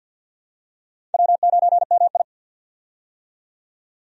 Morse code, heard as a keyed tone, about 51 wpm, 700 Hz. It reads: W9MI
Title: USA Call Signs - 4 Characters - 50wpm